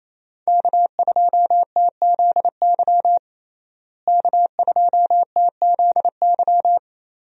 Morse code, heard as a keyed tone, about 28 wpm, 700 Hz. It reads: K2TZY K2TZY
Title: USA Call Signs - 5 Characters - Repeated 3x - 28wpm